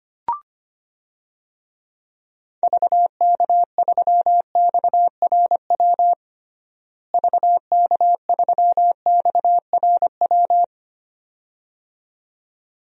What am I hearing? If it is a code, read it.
Morse code, heard as a keyed tone, about 25 wpm, 700 Hz. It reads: VK3XRW VK3XRW
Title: DXCC-100 Call Signs - 6 Characters - Repeated 2x - 25wpm